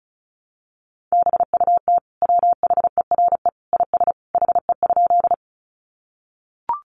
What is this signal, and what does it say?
Morse code, heard as a keyed tone, about 34 wpm, 700 Hz. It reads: BUT WHERE IS HE?
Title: Sentences from Top 200 Words 35wpm